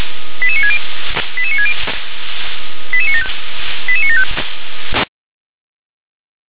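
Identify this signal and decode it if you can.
Five-tone selective call (ZVEI), heard as quick runs of five tones, tones of about 70 ms. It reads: 88960 88960 88975 88975
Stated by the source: ZVEI five-tone selective call 88966 received with an SDR.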